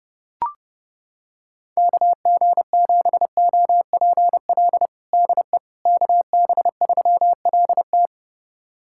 Morse code, heard as a keyed tone, about 32 wpm, 700 Hz. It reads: KG7OPL DE KB3LT
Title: USA Callsign DE Callsign 30wpm